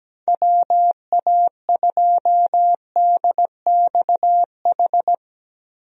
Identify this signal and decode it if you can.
Morse code, heard as a keyed tone, about 17 wpm, 700 Hz. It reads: WA2DXH
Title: USA Call Signs - 6 Characters - Repeated 2x - 17wpm